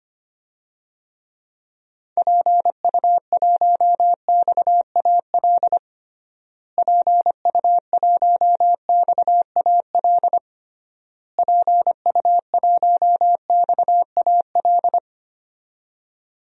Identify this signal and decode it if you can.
Morse code, heard as a keyed tone, about 25 wpm, 700 Hz. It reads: PU1XAL PU1XAL PU1XAL
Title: DXCC-100 Call Signs - 6 Characters - Repeated 3x - 25wpm